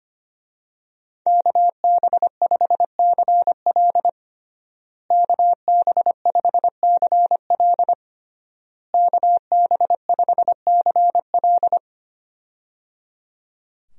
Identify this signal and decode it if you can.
Morse code, heard as a keyed tone, about 25 wpm, 700 Hz. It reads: KB5CL KB5CL KB5CL
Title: USA Call Signs - 5 Characters - Repeated 3x - 25wpm